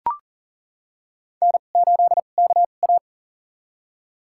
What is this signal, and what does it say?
Morse code, heard as a keyed tone, about 37 wpm, spaced as for 28 wpm, 700 Hz. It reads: N8KA
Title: USA Call Signs - 4 Characters - 2x Character Spacing - 40wpm